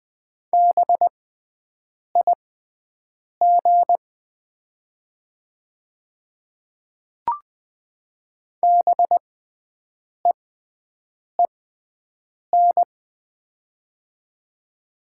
Morse code, heard as a keyed tone, about 20 wpm, spaced as for 7 wpm, 690 Hz. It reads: BIG BEEN
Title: 6x Character Spacing - Top 200 Words - 20wpm